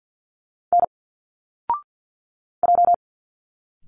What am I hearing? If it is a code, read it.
Morse code, heard as a keyed tone, about 51 wpm, 700 Hz. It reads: N J
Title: Single Letters 50wpm